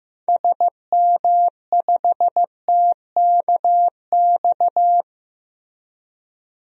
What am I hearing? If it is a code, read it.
Morse code, heard as a keyed tone, about 15 wpm, 700 Hz. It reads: SM5TKX
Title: DXCC-100 Call Signs - 6 Characters - Repeated 2x - 15wpm